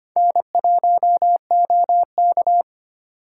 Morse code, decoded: N1OK

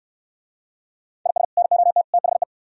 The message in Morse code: IS NOT THE

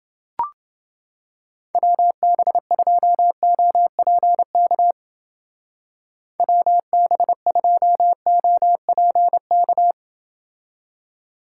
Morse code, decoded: WB2OPK WB2OPK

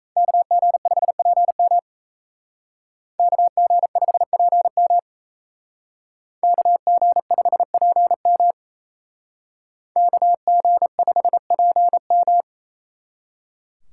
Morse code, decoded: KG5PM KG5PM KG5PM KG5PM

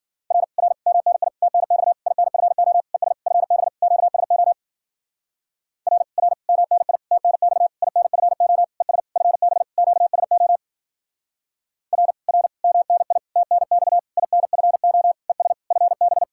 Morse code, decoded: R R MNI TNX INFO ES FB QSO R R MNI TNX INFO ES FB QSO R R MNI TNX INFO ES FB

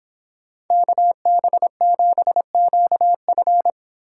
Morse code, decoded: KB7QF

